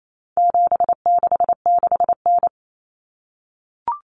Morse code, heard as a keyed tone, about 28 wpm, 700 Hz. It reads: 766N
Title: Common Cut Numbers - 4-Digit 28wpm